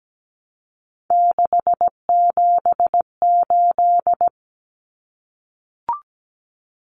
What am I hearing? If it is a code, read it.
Morse code, heard as a keyed tone, about 17 wpm, 700 Hz. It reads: 678
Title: Common Cut Numbers - 3-Digit 17wpm